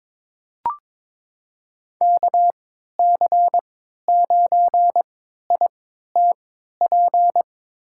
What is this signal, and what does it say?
Morse code, heard as a keyed tone, about 22 wpm, spaced as for 13 wpm, 700 Hz. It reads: KC9ITP
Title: USA Call Signs - 6 Characters - 3x Character Spacing - 22wpm